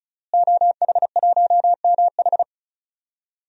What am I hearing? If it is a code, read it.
Morse code, heard as a keyed tone, about 36 wpm, 700 Hz. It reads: OH1MH
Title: DXCC-100 Call Signs - 5 Characters - Repeated 2x - 35wpm